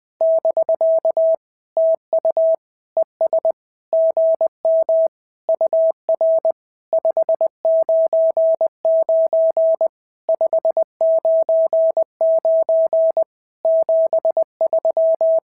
Morse code, heard as a keyed tone, about 20 wpm, 690 Hz. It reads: <BK> TU ES GM UR 599 599 73
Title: SOTA QSO Practice 20wpm